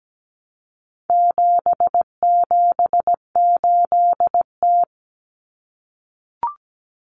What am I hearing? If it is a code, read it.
Morse code, heard as a keyed tone, about 17 wpm, 700 Hz. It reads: 778T
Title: Common Cut Numbers - 4-Digit 17wpm